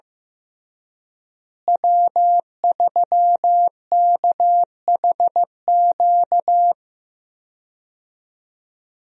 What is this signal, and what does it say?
Morse code, heard as a keyed tone, about 15 wpm, 700 Hz. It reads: W3KHQ
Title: USA Call Signs - 5 Characters - Repeated 2x - 15wpm